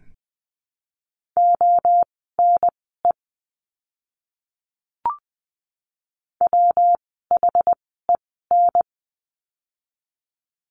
Morse code, heard as a keyed tone, about 19 wpm, spaced as for 14 wpm, 700 Hz. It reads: ONE WHEN
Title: 244 Lesson 39f - Letter z - Words 20wpm